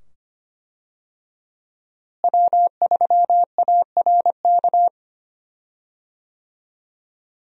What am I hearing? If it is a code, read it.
Morse code, heard as a keyed tone, about 25 wpm, 700 Hz. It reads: W3ARK
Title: USA Call Signs - 5 Characters - Repeated 2x - 25wpm